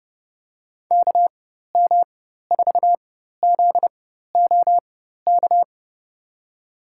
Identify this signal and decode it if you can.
Morse code, heard as a keyed tone, about 30 wpm, spaced as for 14 wpm, 700 Hz. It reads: KM4ZOK
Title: USA Call Signs - 6 Characters - 4x Character Spacing - 30wpm